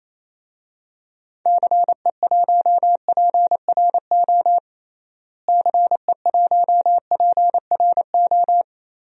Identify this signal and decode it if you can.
Morse code, heard as a keyed tone, about 28 wpm, 700 Hz. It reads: CE1PRO CE1PRO
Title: DXCC-100 Call Signs - 6 Characters - Repeated 2x - 28wpm